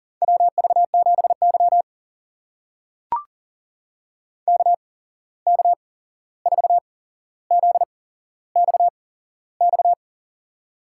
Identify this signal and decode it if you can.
Morse code, heard as a keyed tone, about 40 wpm, spaced as for 15 wpm, 700 Hz. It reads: WV7Y KK4ZXX
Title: USA Call Signs - 8x Character Spacing - 40wpm